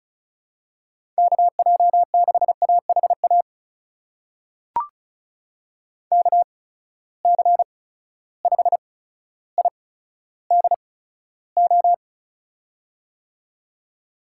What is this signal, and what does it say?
Morse code, heard as a keyed tone, about 34 wpm, spaced as for 15 wpm, 700 Hz. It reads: KJ6AHA KC5IDO
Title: USA Call Signs - 6 Characters - 8x Character Spacing - 35wpm